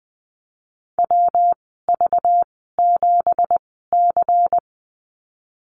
Morse code, heard as a keyed tone, about 20 wpm, spaced as for 15 wpm, 700 Hz. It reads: WV7C
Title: 200 Lesson 32g - Letter k - Call Signs 20wpm